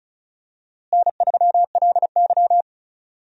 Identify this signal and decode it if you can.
Morse code, heard as a keyed tone, about 36 wpm, 700 Hz. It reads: N3LY
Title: USA Call Signs - 4 Characters - Repeated 2x - 35wpm